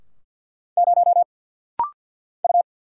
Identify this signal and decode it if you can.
Morse code, heard as a keyed tone, about 50 wpm, 690 Hz. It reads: 0 U
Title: Single Letter-Number Alt - Rapid-Fire 50wpm